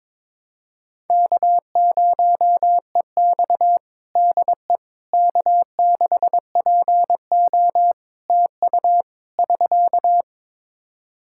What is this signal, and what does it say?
Morse code, heard as a keyed tone, about 22 wpm, 700 Hz. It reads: K0EX DE K6PO TU <SK>
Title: Practice QSO - Exchange 4 - Ending - 22wpm